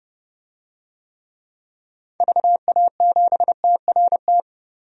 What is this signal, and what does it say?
Morse code, heard as a keyed tone, about 30 wpm, 700 Hz. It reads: VA7TRT
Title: DXCC-100 Call Signs - 6 Characters - 30wpm